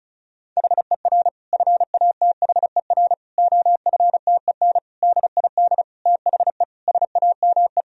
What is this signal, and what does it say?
Morse code, heard as a keyed tone, about 36 wpm, 700 Hz. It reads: HER FATHER OFTEN DID THE SAME
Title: Sentences from Top 1000 Words - Review Part 1 - 35wpm